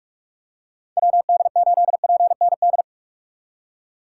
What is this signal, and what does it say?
Morse code, heard as a keyed tone, about 41 wpm, 700 Hz. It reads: WD8PND
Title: USA Call Signs - 6 Characters - 2x Character Spacing - 45wpm